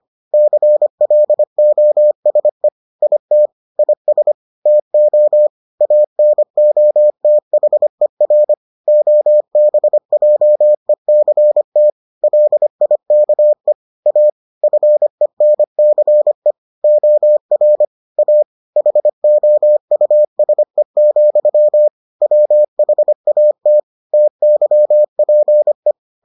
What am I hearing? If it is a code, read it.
Morse code, heard as a keyed tone, about 25 wpm, 600 Hz. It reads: CLOSE IT IS TO ANOTHER OBJECT LIKE A FENCE OR A HOUSE, WHAT TYPE